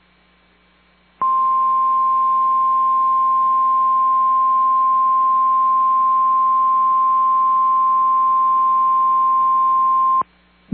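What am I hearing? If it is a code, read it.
Emergency Alert System (SAME) tones (the 1050 Hz attention tone). It is heard from 1.2 to 10.2 s.
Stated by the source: Emergency Alert System log recording from station KMNR.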